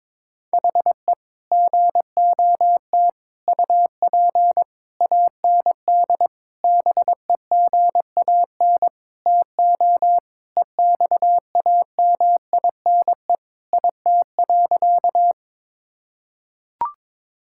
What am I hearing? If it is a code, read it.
Morse code, heard as a keyed tone, about 22 wpm, 700 Hz. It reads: HE GOT UP AND BEGAN TO EXAMINE IT.
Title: Sentences from Top 1600 Words 22wpm